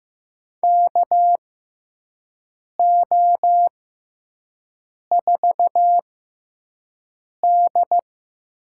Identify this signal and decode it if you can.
Morse code, heard as a keyed tone, about 15 wpm, spaced as for 5 wpm, 700 Hz. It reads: KO4D
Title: USA Call Signs - 6x Character Spacing - 15wpm